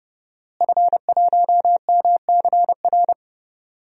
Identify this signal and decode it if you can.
Morse code, heard as a keyed tone, about 32 wpm, 700 Hz. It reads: F1MCR
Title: DXCC-100 Call Signs - 5 Characters - Repeated 2x - 30wpm